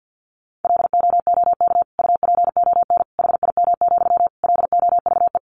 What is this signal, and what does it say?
Morse code, heard as a keyed tone, about 50 wpm, 700 Hz. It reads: LOOK UPON HIM, LOVE
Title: Sentences from Top 1100 Words 50wpm